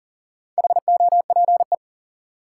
Morse code, heard as a keyed tone, about 40 wpm, 700 Hz. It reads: HOPE
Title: Sentences from Top 1000 Words - New Word Drill - 40wpm